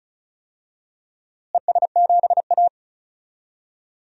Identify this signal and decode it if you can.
Morse code, heard as a keyed tone, about 34 wpm, 700 Hz. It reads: ES7A
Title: DXCC-100 Call Signs - 4 Characters - Repeated 3x - 35wpm